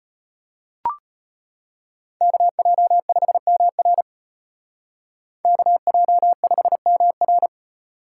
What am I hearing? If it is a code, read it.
Morse code, heard as a keyed tone, about 32 wpm, 700 Hz. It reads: KJ5MR KJ5MR